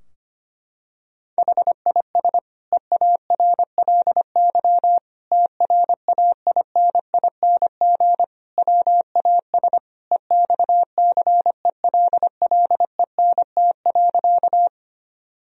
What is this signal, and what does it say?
Morse code, heard as a keyed tone, about 25 wpm, 700 Hz. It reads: HIS EARLY TRAINING WAS EXCELLENT.
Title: Sentences from Top 1000 Words - Review Part 4 - 25wpm